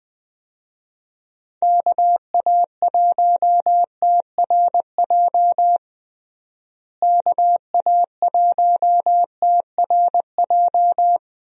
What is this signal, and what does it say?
Morse code, heard as a keyed tone, about 20 wpm, 700 Hz. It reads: KA1TRJ KA1TRJ
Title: USA Call Signs - 6 Characters - Repeated 2x - 20wpm